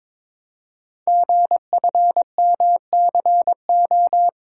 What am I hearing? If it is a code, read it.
Morse code, heard as a keyed tone, about 22 wpm, 700 Hz. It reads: GFMCO
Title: Code Groups - 5 Letters - 22wpm